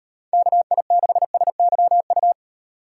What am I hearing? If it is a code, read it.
Morse code, heard as a keyed tone, about 38 wpm, 700 Hz. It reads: KI6SYU